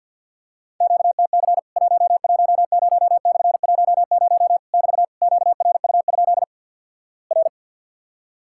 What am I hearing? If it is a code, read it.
Morse code, heard as a keyed tone, about 50 wpm, 700 Hz. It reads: QTC 110/10 <BT> QRU? R
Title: Worked All Europe DX Contest - QTC Exchange - 50wpm